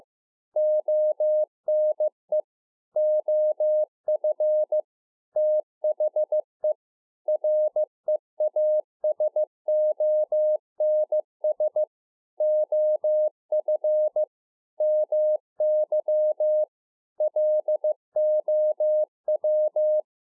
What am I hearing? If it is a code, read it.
Morse code, heard as a keyed tone, about 15 wpm, 610 Hz. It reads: ONE OF THE REASONS OF MY LOW
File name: 15WPM